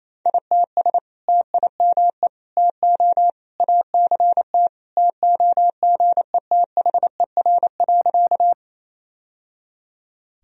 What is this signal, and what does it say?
Morse code, heard as a keyed tone, about 28 wpm, 700 Hz. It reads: ITS TIME TO ACT TOGETHER.
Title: Sentences from Top 1000 Words - Review Part 1 - 28wpm